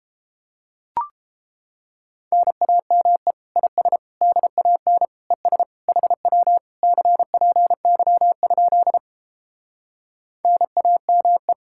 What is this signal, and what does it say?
Morse code, heard as a keyed tone, about 31 wpm, 700 Hz. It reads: NAME IS DAN ES HW CPY? NAME